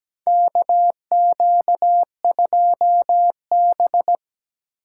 Morse code, heard as a keyed tone, about 17 wpm, 700 Hz. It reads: KQ2B
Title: USA Call Signs - 4 Characters - 6x Character Spacing - 17wpm